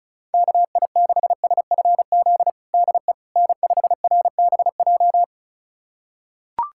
Morse code, heard as a keyed tone, about 34 wpm, 700 Hz. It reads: KI6SFZ DE N5RBJ
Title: USA Callsign DE Callsign 35wpm